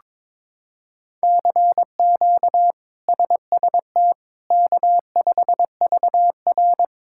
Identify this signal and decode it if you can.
Morse code, heard as a keyed tone, about 22 wpm, 700 Hz. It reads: CQ SST K5VR
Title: K1USN SST QSO Practice - 22wpm